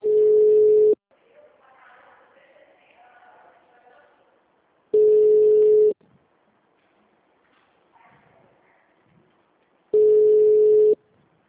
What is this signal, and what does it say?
Ringback tone (Europe, 425 Hz), with speech.